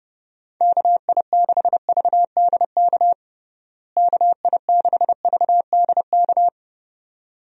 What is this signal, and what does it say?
Morse code, heard as a keyed tone, about 30 wpm, 700 Hz. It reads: KI6VDK KI6VDK
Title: USA Call Signs - 6 Characters - Repeated 2x - 30wpm